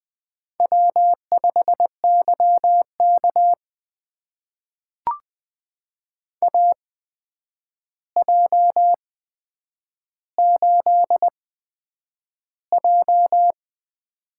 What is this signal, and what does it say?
Morse code, heard as a keyed tone, about 20 wpm, spaced as for 9 wpm, 700 Hz. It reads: W5YK AJ8J
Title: USA Call Signs - 4 Characters - 8x Character Spacing - 20wpm